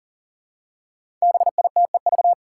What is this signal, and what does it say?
Morse code, heard as a keyed tone, about 40 wpm, 700 Hz. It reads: BITEV
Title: Five Challenging Letter-Number Combinations 40wpm